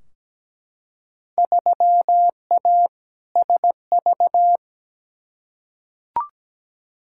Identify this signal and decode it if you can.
Morse code, heard as a keyed tone, about 17 wpm, 700 Hz. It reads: 3A SV
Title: Field Day Exchanges - 17wpm